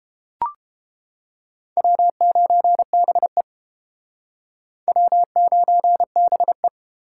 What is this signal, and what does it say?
Morse code, heard as a keyed tone, about 30 wpm, 700 Hz. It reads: W9BE W9BE